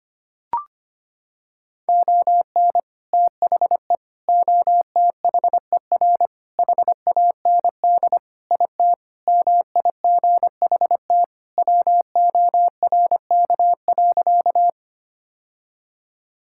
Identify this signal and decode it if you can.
Morse code, heard as a keyed tone, about 25 wpm, 700 Hz. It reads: ON THE OTHER HAND IT MIGHT WORK.
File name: practice / 25wpm / Sentences with 3-Word Phrases - 25wpm